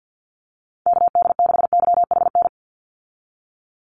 Morse code, heard as a keyed tone, about 50 wpm, 700 Hz. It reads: KD6YHN
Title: USA Call Signs 50wpm